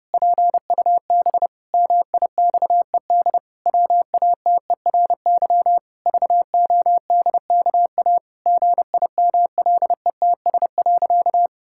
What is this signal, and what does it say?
Morse code, heard as a keyed tone, about 30 wpm, 700 Hz. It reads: PUB MIXED WATERY VODKA GIMLETS.
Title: Pangram Challenge v2 - 30wpm